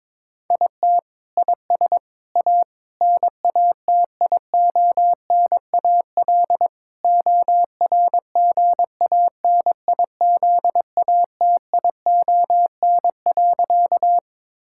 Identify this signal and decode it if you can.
Morse code, heard as a keyed tone, about 22 wpm, 700 Hz. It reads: IT IS A NATIONAL ORGANIZATION.
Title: Sentences from Top 1000 Words - Review Part 5 - 22wpm